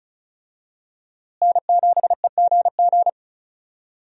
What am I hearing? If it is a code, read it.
Morse code, heard as a keyed tone, about 36 wpm, 700 Hz. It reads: N7EGG
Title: USA Call Signs - 5 Characters - Repeated 3x - 35wpm